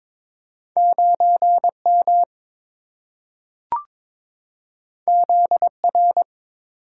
Morse code, heard as a keyed tone, about 23 wpm, 700 Hz. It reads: 9M ZR
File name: practice / 22wpm / Two Letter-Number Combinations - 22wpm